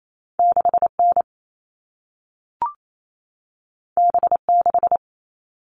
Morse code, heard as a keyed tone, about 28 wpm, 700 Hz. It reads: 6N B6
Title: All Cut Numbers - 2-Digit 28wpm